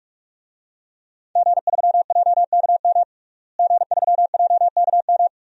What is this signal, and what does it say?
Morse code, heard as a keyed tone, about 46 wpm, 700 Hz. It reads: G3JKM G3JKM
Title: DXCC-100 Call Signs - 5 Characters - Repeated 2x - 45wpm